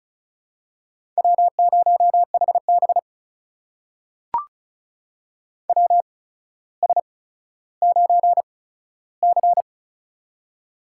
Morse code, heard as a keyed tone, about 36 wpm, spaced as for 15 wpm, 700 Hz. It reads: W0HB WS9C
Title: USA Call Signs - 8x Character Spacing - 35wpm